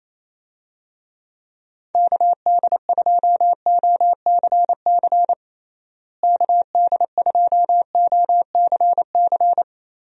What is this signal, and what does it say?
Morse code, heard as a keyed tone, about 28 wpm, 700 Hz. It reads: KD2OCC KD2OCC
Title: USA Call Signs - 6 Characters - Repeated 2x - 28wpm